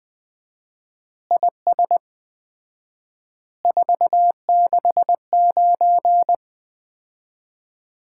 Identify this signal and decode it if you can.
Morse code, heard as a keyed tone, about 20 wpm, 700 Hz. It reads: IS 469
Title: Common QSO Phrases - 4x Word-Spacing - 20wpm